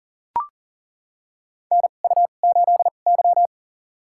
Morse code, heard as a keyed tone, about 38 wpm, spaced as for 28 wpm, 700 Hz. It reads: NU8Y
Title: USA Call Signs - 4 Characters - 2x Character Spacing - 40wpm